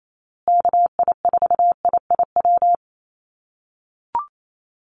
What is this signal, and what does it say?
Morse code, heard as a keyed tone, about 28 wpm, 700 Hz. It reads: KI4IIW
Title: USA Call Signs Repeated 3x 28wpm